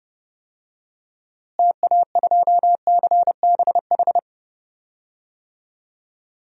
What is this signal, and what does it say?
Morse code, heard as a keyed tone, about 30 wpm, 700 Hz. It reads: TA2CBH
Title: DXCC-100 Call Signs - 6 Characters - Repeated 2x - 30wpm